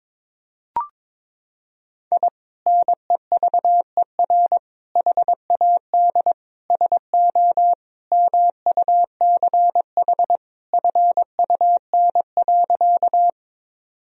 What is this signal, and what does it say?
Morse code, heard as a keyed tone, about 21 wpm, 700 Hz. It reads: I NEVER HAD SO MUCH FUN.
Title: Sentences from Top 1000 Words - Review Part 4 - 22wpm